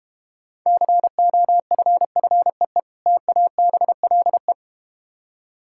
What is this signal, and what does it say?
Morse code, heard as a keyed tone, about 31 wpm, 700 Hz. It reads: COFFEE TABLE
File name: Easy Words - Speed-Racing with Hints - 25wpm